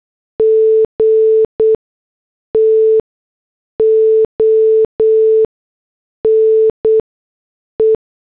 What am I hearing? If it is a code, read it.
Morse code, heard as a keyed tone, about 8 wpm, spaced as for 6 wpm, 440 Hz. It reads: G TO NE